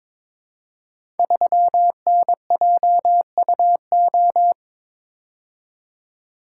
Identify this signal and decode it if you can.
Morse code, heard as a keyed tone, about 21 wpm, 700 Hz. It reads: 3NJUO
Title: Code Groups - 5 Letter:Numbers - 22wpm